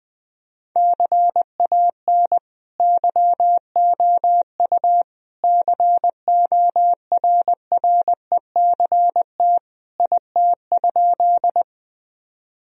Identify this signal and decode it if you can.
Morse code, heard as a keyed tone, about 20 wpm, 700 Hz. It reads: CAN YOU CORRECT IT?
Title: Sentences from Top 1000 Words - Review Part 2 - 20wpm